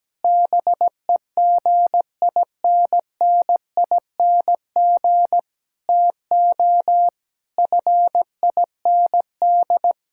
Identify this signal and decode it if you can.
Morse code, heard as a keyed tone, about 17 wpm, 700 Hz. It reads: BEGINNING TO FIND
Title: Sentences from Top 1000 Words - Review Part 5 - 17wpm